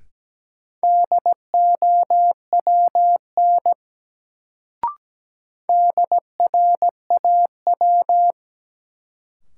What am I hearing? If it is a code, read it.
Morse code, heard as a keyed tone, about 17 wpm, 700 Hz. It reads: DOWN DRAW
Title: Sentences from Top 300 Words - New Word Drill - 17wpm